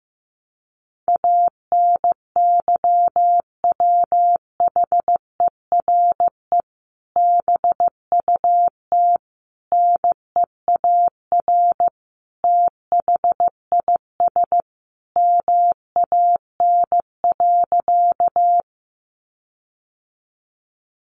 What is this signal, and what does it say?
Morse code, heard as a keyed tone, about 15 wpm, 700 Hz. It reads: ANYWHERE BUT NEAR THIS MAN.
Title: Sentences from Top 1400 Words 15wpm